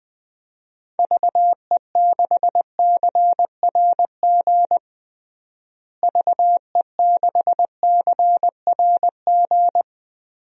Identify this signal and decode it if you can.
Morse code, heard as a keyed tone, about 20 wpm, 700 Hz. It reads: VE6CRG VE6CRG
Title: DXCC-100 Call Signs - 6 Characters - Repeated 3x - 20wpm